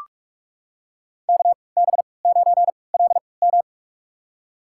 Morse code, heard as a keyed tone, about 46 wpm, spaced as for 25 wpm, 700 Hz. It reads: KB9LM
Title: USA Call Signs - 5 Characters - 3x Character Spacing - 45wpm